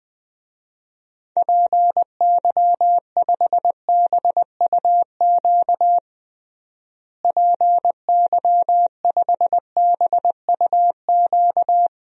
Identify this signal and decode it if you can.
Morse code, heard as a keyed tone, about 20 wpm, 700 Hz. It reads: PY5BUQ PY5BUQ
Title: DXCC-100 Call Signs - 6 Characters - Repeated 3x - 20wpm